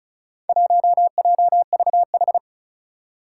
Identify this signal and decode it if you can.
Morse code, heard as a keyed tone, about 36 wpm, 700 Hz. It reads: 1JVH
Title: Four Challenging Letter-Number Combinations 35wpm